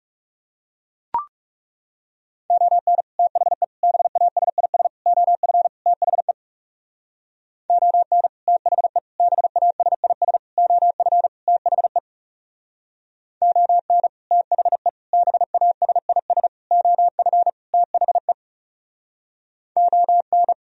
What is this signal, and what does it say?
Morse code, heard as a keyed tone, about 36 wpm, 700 Hz. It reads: ON THE BASIS OF THE ON THE BASIS OF THE ON THE BASIS OF THE ON
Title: Speed-Racing - Sets of 5 Words 30wpm